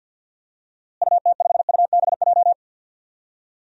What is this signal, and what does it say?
Morse code, heard as a keyed tone, about 52 wpm, 700 Hz. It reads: UT5UBJ
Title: DXCC-100 Call Signs - 6 Characters - Repeated 3x - 50wpm